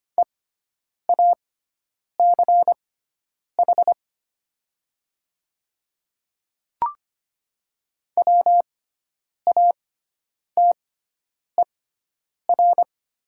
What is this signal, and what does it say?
Morse code, heard as a keyed tone, about 25 wpm, spaced as for 9 wpm, 700 Hz. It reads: EACH WATER
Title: 6x Character Spacing - Top 100 Words - 25wpm